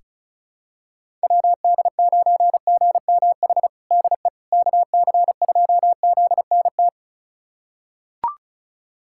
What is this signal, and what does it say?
Morse code, heard as a keyed tone, about 36 wpm, 700 Hz. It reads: WD9GMH DE KC2ZNT
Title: USA Callsign DE Callsign 35wpm